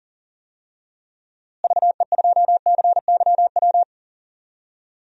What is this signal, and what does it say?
Morse code, heard as a keyed tone, about 40 wpm, 700 Hz. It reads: VE2CYW
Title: DXCC-100 Call Signs - 6 Characters - Repeated 3x - 40wpm